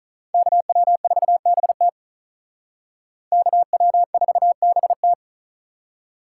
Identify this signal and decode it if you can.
Morse code, heard as a keyed tone, about 45 wpm, 700 Hz. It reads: KW4BT KW4BT